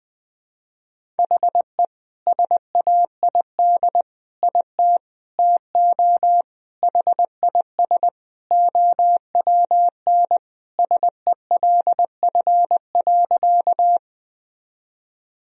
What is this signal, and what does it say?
Morse code, heard as a keyed tone, about 19 wpm, 700 Hz. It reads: HE SAID IT TO HIS OWN SELF.
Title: Sentences from Top 1000 Words - Review Part 1 - 20wpm